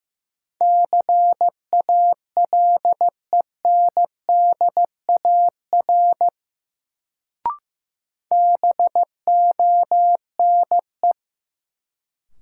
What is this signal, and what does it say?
Morse code, heard as a keyed tone, about 15 wpm, 700 Hz. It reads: CALENDAR BONE
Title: Sentences from Top 1600 Words - New Word Drill Alt - 15wpm